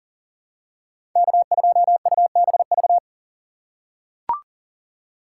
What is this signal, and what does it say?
Morse code, heard as a keyed tone, about 40 wpm, 700 Hz. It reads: K2UBV
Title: USA Call Signs - 5 Characters - 2x Character Spacing - 40wpm